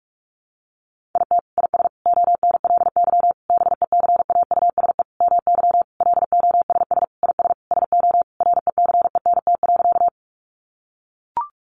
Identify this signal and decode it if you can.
Morse code, heard as a keyed tone, about 44 wpm, 700 Hz. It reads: IT IS ONLY BECAUSE MY LOSS IS SO RECENT.
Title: Sentences from Top 1400 Words 45wpm